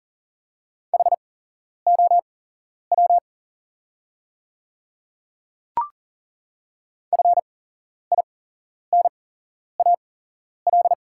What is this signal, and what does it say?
Morse code, heard as a keyed tone, about 40 wpm, spaced as for 11 wpm, 700 Hz. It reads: HOW FINAL
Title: 8x Character Spacing - Top 500 Words - 40wpm